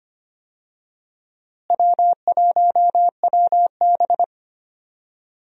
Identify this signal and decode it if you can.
Morse code, heard as a keyed tone, about 25 wpm, 700 Hz. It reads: W1WB
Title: USA Call Signs - 4 Characters - Repeated 3x - 25wpm